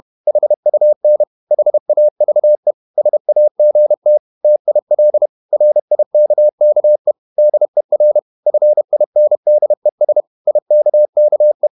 Morse code, heard as a keyed tone, about 31 wpm, 600 Hz. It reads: HUN HAVE SAGT TIL RIKKE DER FINDES IKKE
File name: Ferie QRQ 31wpm